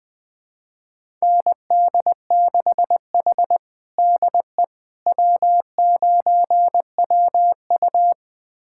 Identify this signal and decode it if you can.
Morse code, heard as a keyed tone, about 20 wpm, 700 Hz. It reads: ND6H DE W9WU
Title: USA Callsign DE Callsign - 4 Characters - 20wpm